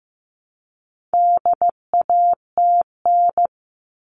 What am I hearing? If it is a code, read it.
Morse code, heard as a keyed tone, about 15 wpm, 700 Hz. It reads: DATN
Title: All Cut Numbers - 4-Digit 15wpm